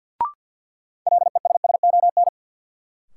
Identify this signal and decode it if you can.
Morse code, heard as a keyed tone, about 52 wpm, 700 Hz. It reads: LESSON
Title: Sentences from Top 1200 Words - New Word Drill - 50wpm